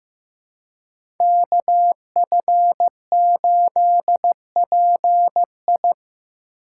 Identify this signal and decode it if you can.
Morse code, heard as a keyed tone, about 15 wpm, 700 Hz. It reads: KF8PI
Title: USA Call Signs - 5 Characters - 4x Character Spacing - 15wpm